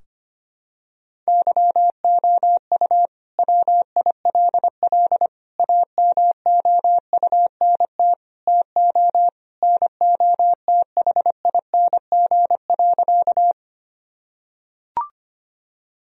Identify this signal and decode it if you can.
Morse code, heard as a keyed tone, about 25 wpm, 700 Hz. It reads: YOU WILL AMOUNT TO NOTHING.
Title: Sentences from Top 1000 Words - Review Part 4 - 25wpm